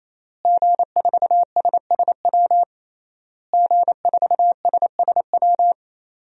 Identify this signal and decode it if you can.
Morse code, heard as a keyed tone, about 28 wpm, 700 Hz. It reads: G4SSW G4SSW
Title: DXCC-100 Call Signs - 5 Characters - Repeated 3x - 28wpm